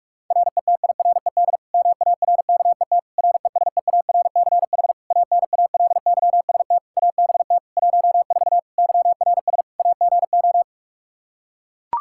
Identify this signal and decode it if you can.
Morse code, heard as a keyed tone, about 41 wpm, 700 Hz. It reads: RETIRED MARKET RESEARCH ANALYST ABT 14 YRS AGO
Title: Practice QSO - Exchange 3 - More Info - 45wpm